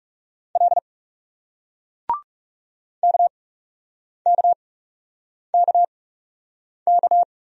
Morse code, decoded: L KKKK

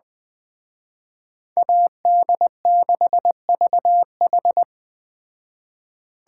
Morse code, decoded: AD6VH

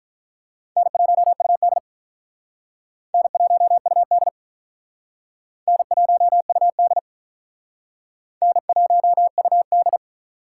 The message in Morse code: N1UD N1UD N1UD N1UD

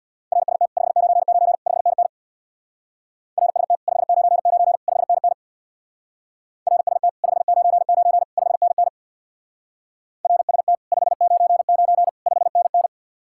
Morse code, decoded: RST 599 5NN RST 599 5NN RST 599 5NN RST 599 5NN

